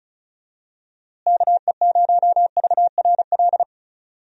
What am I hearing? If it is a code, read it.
Morse code, heard as a keyed tone, about 34 wpm, 700 Hz. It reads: KE0VRL
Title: USA Call Signs - 6 Characters - Repeated 3x - 35wpm